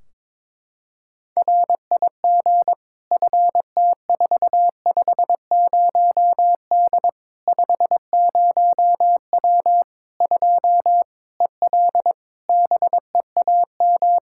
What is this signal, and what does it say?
Morse code, heard as a keyed tone, about 22 wpm, 700 Hz. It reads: RIG FT450D 50W 2 EL BEAM
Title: Practice QSO - Exchange 2 - Info - 22wpm